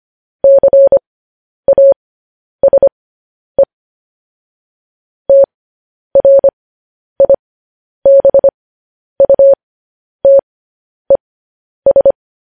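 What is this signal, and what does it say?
Morse code, heard as a keyed tone, about 25 wpm, spaced as for 10 wpm, 550 Hz. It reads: CASE TRIBUTES
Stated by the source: Morse code news headlines at 10.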